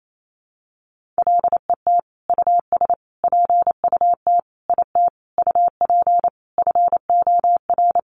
Morse code, heard as a keyed tone, about 28 wpm, 700 Hz. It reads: LET US PUT IT UP FOR
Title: Sentences from Top 1400 Words 28wpm